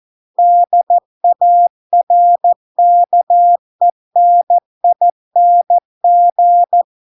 Morse code, heard as a keyed tone, about 14 wpm, 700 Hz. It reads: DARKENING